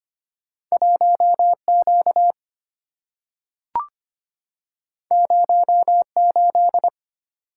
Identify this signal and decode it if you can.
Morse code, heard as a keyed tone, about 25 wpm, 700 Hz. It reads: 1Q 08
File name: Two Letter-Number Combinations - 25wpm